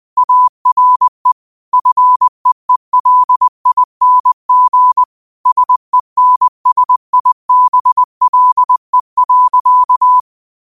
Morse code, heard as a keyed tone, about 20 wpm, 1000 Hz. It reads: ARE FEELING SENSIBLE.
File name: 20wpm